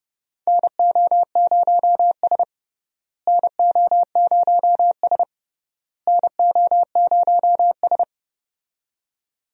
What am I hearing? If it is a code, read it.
Morse code, heard as a keyed tone, about 29 wpm, 700 Hz. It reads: NO0S NO0S NO0S
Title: USA Call Signs - 4 Characters - Repeated 3x - 30wpm